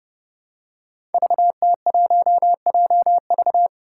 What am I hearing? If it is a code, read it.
Morse code, heard as a keyed tone, about 28 wpm, 700 Hz. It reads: VT1JV
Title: Five Challenging Letter-Number Combinations 30wpm